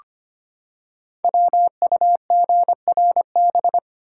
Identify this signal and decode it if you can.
Morse code, heard as a keyed tone, about 25 wpm, 700 Hz. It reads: WUGRB
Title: Code Groups - 5 Letters - 25wpm